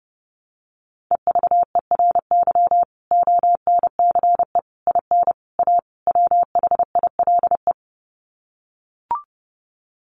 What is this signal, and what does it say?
Morse code, heard as a keyed tone, about 30 wpm, 700 Hz. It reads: EVERY ONCE IN A WHILE
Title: Sets of 5 Words 30wpm